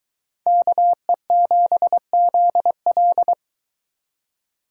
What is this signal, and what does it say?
Morse code, heard as a keyed tone, about 21 wpm, 700 Hz. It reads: KE7ZL